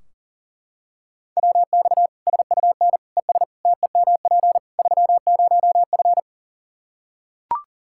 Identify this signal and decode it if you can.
Morse code, heard as a keyed tone, about 37 wpm, 700 Hz. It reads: WX SUN ES TEMP 30F
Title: Practice QSO - Exchange 2 - Info - 40wpm